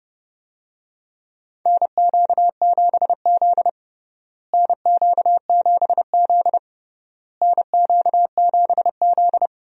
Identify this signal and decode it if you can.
Morse code, heard as a keyed tone, about 30 wpm, 700 Hz. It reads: NQ7Z NQ7Z NQ7Z
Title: USA Call Signs - 4 Characters - Repeated 3x - 30wpm